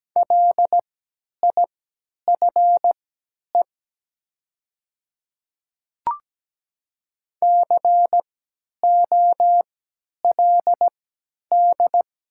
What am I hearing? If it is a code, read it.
Morse code, heard as a keyed tone, about 17 wpm, spaced as for 10 wpm, 700 Hz. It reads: LIFE COLD
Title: 3x Character Spacing - Top 500 Words - 17wpm